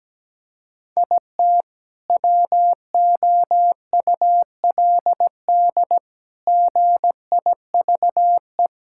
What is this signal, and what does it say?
Morse code, heard as a keyed tone, about 17 wpm, 700 Hz. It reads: IT WOULD GIVE
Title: Sentences from Top 1600 Words 17wpm